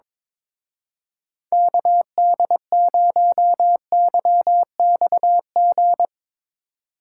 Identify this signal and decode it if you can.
Morse code, heard as a keyed tone, about 24 wpm, 700 Hz. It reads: KD0YXG
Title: Speed Racing - USA 2x3 Call Signs 22wpm